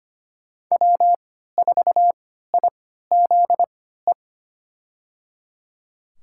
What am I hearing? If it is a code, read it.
Morse code, heard as a keyed tone, about 25 wpm, spaced as for 14 wpm, 700 Hz. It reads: W4IZE
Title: USA Call Signs - 3x Character Spacing - 25wpm